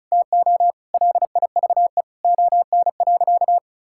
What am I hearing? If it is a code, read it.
Morse code, heard as a keyed tone, about 34 wpm, 700 Hz. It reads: TO LIVE ON.
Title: Sentences from Top 1000 Words - Review Part 6 - 35wpm